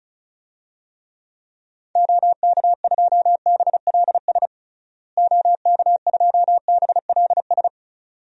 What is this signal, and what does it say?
Morse code, heard as a keyed tone, about 34 wpm, 700 Hz. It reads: OK2BLS OK2BLS
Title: DXCC-100 Call Signs - 6 Characters - Repeated 2x - 35wpm